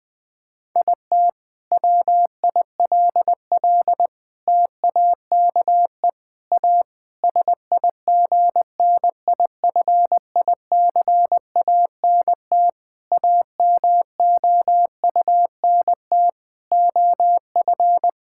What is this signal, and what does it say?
Morse code, heard as a keyed tone, about 21 wpm, 700 Hz. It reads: IT WILL TAKE A SIGNIFICANT AMOUNT OF
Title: Sentences from Top 1000 Words - Review Part 6 - 20wpm